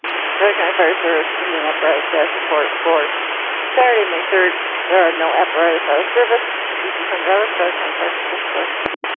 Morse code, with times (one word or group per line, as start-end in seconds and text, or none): none